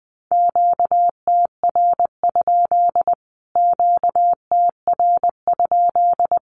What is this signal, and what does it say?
Morse code, heard as a keyed tone, about 20 wpm, 700 Hz. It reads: QTR? QTR?
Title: Q-Codes Repeated 3x 20wpm